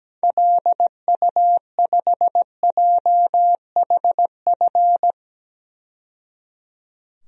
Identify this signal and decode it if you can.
Morse code, heard as a keyed tone, about 17 wpm, 700 Hz. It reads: LU5JHF
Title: DXCC-100 Call Signs - 6 Characters - 17wpm